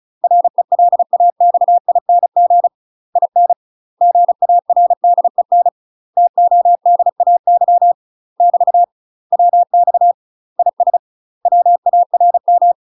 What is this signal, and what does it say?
Morse code, heard as a keyed tone, about 35 wpm, 700 Hz. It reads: RELAXING IN GARDEN TODAY <BT> WX IS WARM
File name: UK MASTER QSOs TRAINING / 35WPM / UK MASTER QSOs 0032